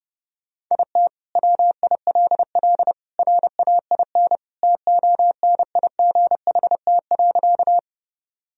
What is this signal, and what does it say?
Morse code, heard as a keyed tone, about 32 wpm, 700 Hz. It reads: IT WILL RAIN TONIGHT.
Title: Sentences from Top 1000 Words - Review Part 1 - 30wpm